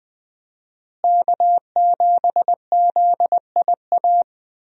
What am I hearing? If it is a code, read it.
Morse code, heard as a keyed tone, about 20 wpm, 700 Hz. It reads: K7ZIA
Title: USA Call Signs - 5 Characters - Repeated 2x - 20wpm